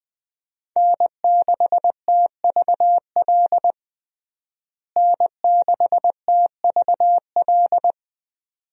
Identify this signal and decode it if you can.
Morse code, heard as a keyed tone, about 20 wpm, 700 Hz. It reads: N6TVL N6TVL
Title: USA Call Signs - 5 Characters - Repeated 2x - 20wpm